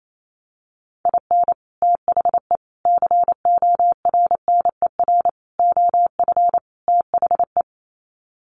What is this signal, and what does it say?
Morse code, heard as a keyed tone, about 28 wpm, 700 Hz. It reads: IN THE CORNER OF THE
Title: Sets of 5 Words Encore 28wpm